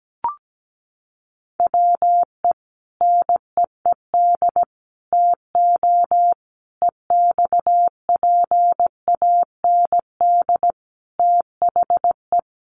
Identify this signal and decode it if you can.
Morse code, heard as a keyed tone, about 17 wpm, 700 Hz. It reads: WE NEED TO EXPAND THE
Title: Sentences from Top 1400 Words 17wpm